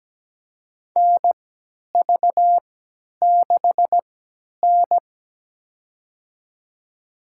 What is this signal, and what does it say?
Morse code, heard as a keyed tone, about 17 wpm, spaced as for 10 wpm, 700 Hz. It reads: NV6N
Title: USA Call Signs - 3x Character Spacing - 17wpm